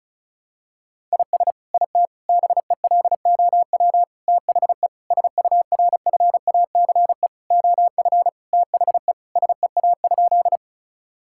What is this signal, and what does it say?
Morse code, heard as a keyed tone, about 34 wpm, 700 Hz. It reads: IS IT BELOW THE SURFACE OF THE SEA?
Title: Sentences from Top 1000 Words - Review Part 5 - 35wpm